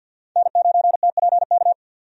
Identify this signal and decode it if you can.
Morse code, heard as a keyed tone, about 50 wpm, 700 Hz. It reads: N9TPK
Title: USA Call Signs - 5 Characters - 50wpm